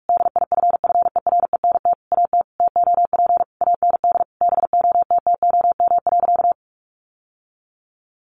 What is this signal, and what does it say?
Morse code, heard as a keyed tone, about 44 wpm, 700 Hz. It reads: DIFFERENT AT TOP AND BOTTOM.
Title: Sentences from Top 1400 Words 45wpm